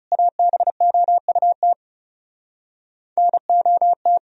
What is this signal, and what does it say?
Morse code, heard as a keyed tone, about 40 wpm, 700 Hz. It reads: ABOUT NOT